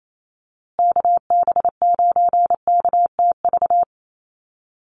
Morse code, heard as a keyed tone, about 28 wpm, 700 Hz. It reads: KB9KTV
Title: USA Call Signs Repeated 2x 28wpm